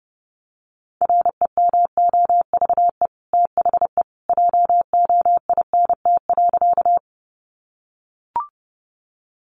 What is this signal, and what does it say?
Morse code, heard as a keyed tone, about 30 wpm, 700 Hz. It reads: REMOVE THE JOINT.
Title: Sentences from Top 1300 Words 30wpm